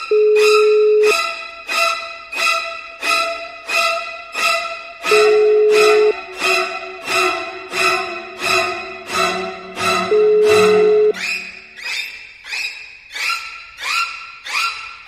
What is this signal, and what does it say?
Ringback tone (Europe, 425 Hz), with noise.